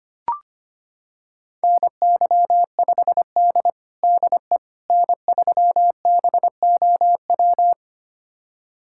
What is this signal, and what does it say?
Morse code, heard as a keyed tone, about 25 wpm, 700 Hz. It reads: NY5D DE N3BOW
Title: USA Callsign DE Callsign 25wpm